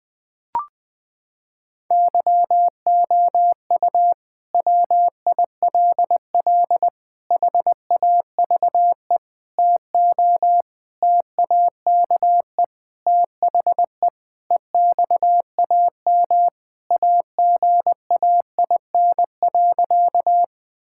Morse code, heard as a keyed tone, about 20 wpm, 700 Hz. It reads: YOU WILL HAVE TO TAKE THE EXAM AGAIN.
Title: Sentences from Top 1000 Words - Review Part 5 - 20wpm